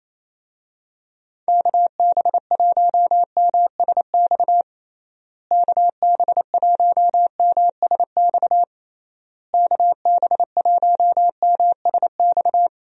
Morse code, heard as a keyed tone, about 28 wpm, 700 Hz. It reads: KB1MSX KB1MSX KB1MSX
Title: USA Call Signs - 6 Characters - Repeated 3x - 28wpm